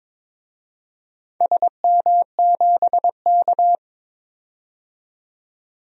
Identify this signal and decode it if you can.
Morse code, heard as a keyed tone, about 22 wpm, 700 Hz. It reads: SM7K
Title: DXCC-100 Call Signs - 4 Characters - 22wpm